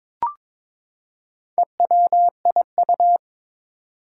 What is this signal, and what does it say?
Morse code, heard as a keyed tone, about 22 wpm, 700 Hz. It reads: EWIU
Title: Four Challenging Letter-Number Combinations 22wpm